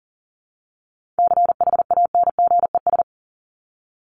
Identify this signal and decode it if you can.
Morse code, heard as a keyed tone, about 40 wpm, 700 Hz. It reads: CHANGES
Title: ES:S Suffix - Top 100 Words with Base Words 40wpm